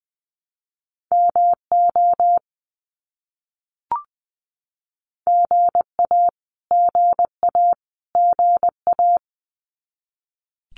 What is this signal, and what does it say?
Morse code, heard as a keyed tone, about 20 wpm, 700 Hz. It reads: MO GA GA GA
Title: US States Repeated 3x 20wpm